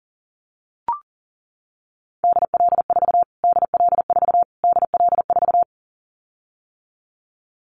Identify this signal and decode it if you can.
Morse code, heard as a keyed tone, about 37 wpm, 700 Hz. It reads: DL4 DL4 DL4
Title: Callsign Prefixes for Top 25 DXCC - Repeated 3x - 40wpm